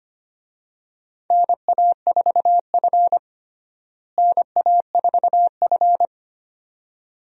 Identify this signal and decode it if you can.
Morse code, heard as a keyed tone, about 25 wpm, 700 Hz. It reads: NA4F NA4F
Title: USA Call Signs - 4 Characters - Repeated 2x - 25wpm